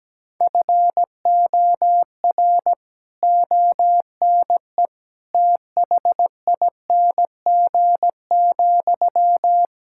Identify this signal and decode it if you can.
Morse code, heard as a keyed tone, about 17 wpm, 700 Hz. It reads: FOR ONE THING,
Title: Sentences from Top 1000 Words - Review Part 6 - 17wpm